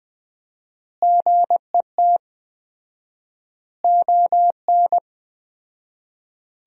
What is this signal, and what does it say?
Morse code, heard as a keyed tone, about 20 wpm, 700 Hz. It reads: GET ON
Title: 4x Word-Spacing - Sentences from Top 200 Words - 20wpm